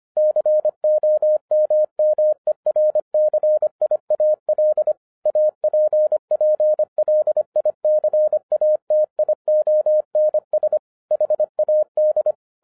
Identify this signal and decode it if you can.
Morse code, heard as a keyed tone, about 25 wpm, 610 Hz. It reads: COMMERCIAL APPLICATIONS HAD